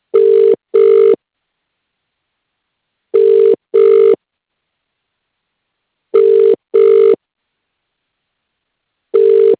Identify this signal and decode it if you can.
Ringback tone (UK), 400 + 450 Hz.